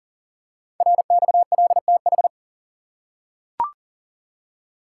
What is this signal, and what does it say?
Morse code, heard as a keyed tone, about 40 wpm, 700 Hz. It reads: RXLTH
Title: Code Groups - 5 Letters - 40wpm